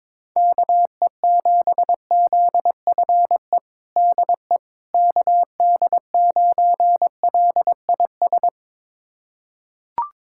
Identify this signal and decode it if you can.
Morse code, heard as a keyed tone, about 22 wpm, 700 Hz. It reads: KE7ZFE DE KD9LIS
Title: USA Callsign DE Callsign 22wpm